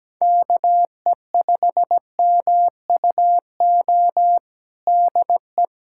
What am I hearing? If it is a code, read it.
Morse code, heard as a keyed tone, about 17 wpm, 700 Hz. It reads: KE5MUO DE
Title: K1USN SST QSO Practice 17wpm